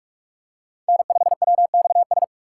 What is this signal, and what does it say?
Morse code, heard as a keyed tone, about 46 wpm, 700 Hz. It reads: N5WXS
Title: USA Call Signs - 5 Characters - 4x Character Spacing - 45wpm